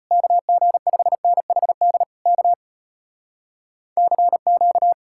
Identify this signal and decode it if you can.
Morse code, heard as a keyed tone, about 33 wpm, 700 Hz. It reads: KG5NHD K CQ